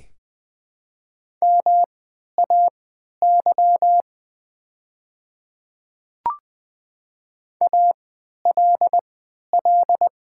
Morse code, heard as a keyed tone, about 21 wpm, spaced as for 12 wpm, 700 Hz. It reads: MAY ALL
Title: 3x Character Spacing - Top 100 Words - 20wpm